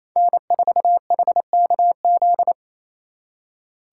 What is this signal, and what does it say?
Morse code, heard as a keyed tone, about 36 wpm, 700 Hz. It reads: N4HKZ